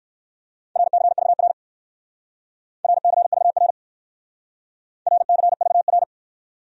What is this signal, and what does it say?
Morse code, heard as a keyed tone, about 56 wpm, 700 Hz. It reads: RCVR RCVR RCVR